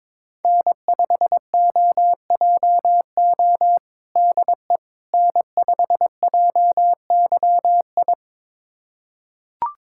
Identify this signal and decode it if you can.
Morse code, heard as a keyed tone, about 22 wpm, 700 Hz. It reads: N5OJO DE N5JYI
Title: USA Callsign DE Callsign - 5 Characters - 22wpm